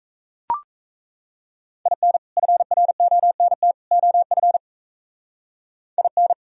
Morse code, heard as a keyed tone, about 36 wpm, 700 Hz. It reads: IN FRONT OF IN